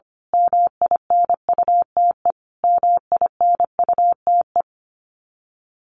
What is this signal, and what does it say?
Morse code, heard as a keyed tone, about 25 wpm, 700 Hz. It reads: MINUTE MINUTE
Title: Top 500 Words Repeated 25wpm